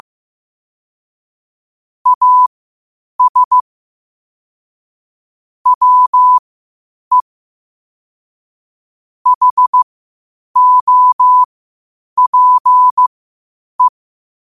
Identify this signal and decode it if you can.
Morse code, heard as a keyed tone, about 15 wpm, spaced as for 8 wpm, 1000 Hz. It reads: AS WE HOPE
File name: qotdfiles / 8wpm